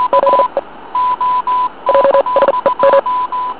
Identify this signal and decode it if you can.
Morse code, heard as a keyed tone, about 40 wpm, spaced as for 32 wpm, 550 Hz. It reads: ZE JSEM